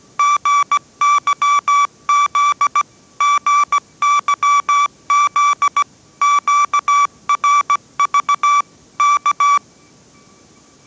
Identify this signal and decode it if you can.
Morse code, heard as a keyed tone, about 17 wpm, 1160 Hz. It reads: GYZ GYZ QRV K
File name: 18wpm